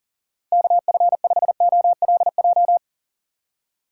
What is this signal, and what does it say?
Morse code, heard as a keyed tone, about 40 wpm, 700 Hz. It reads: KF5OLJ
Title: USA Call Signs - 6 Characters - 6x Character Spacing - 40wpm